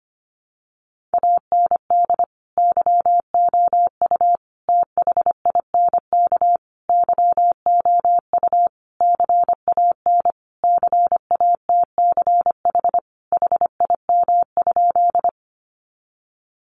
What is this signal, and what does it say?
Morse code, heard as a keyed tone, about 25 wpm, 700 Hz. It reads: AND YOU THINK YOU CAN CATCH HIM?
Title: Sentences from Top 1500 Words 25wpm